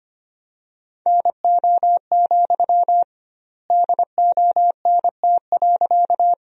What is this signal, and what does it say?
Morse code, heard as a keyed tone, about 25 wpm, 700 Hz. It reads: NO, DONT.
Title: 2x Word-Spacing - Sentences from Top 300 Words - 25wpm